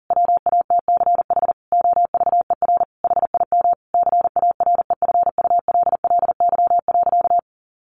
Morse code, heard as a keyed tone, about 40 wpm, 700 Hz. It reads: WATCH OVER HIM CAREFULLY.
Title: Sentences from Top 1500 Words 40wpm